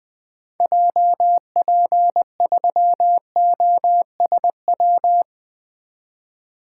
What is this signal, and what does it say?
Morse code, heard as a keyed tone, about 21 wpm, 700 Hz. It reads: JP3OSW
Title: 2023 SOTA Activator Call Signs - 20wpm